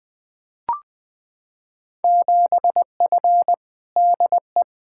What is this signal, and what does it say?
Morse code, heard as a keyed tone, about 21 wpm, 700 Hz. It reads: 7F DE
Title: Field Day Exchanges Encore - 20wpm